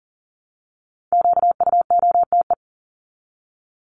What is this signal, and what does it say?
Morse code, heard as a keyed tone, about 39 wpm, 700 Hz. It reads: QUOTE
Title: Top 1751-2000 Words 40wpm